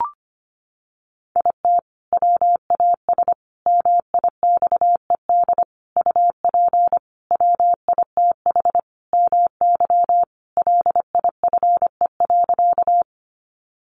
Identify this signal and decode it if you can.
Morse code, heard as a keyed tone, about 25 wpm, 700 Hz. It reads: IT WAS MIXED UP WITH MY LIFE.
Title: Sentences from Top 1500 Words 25wpm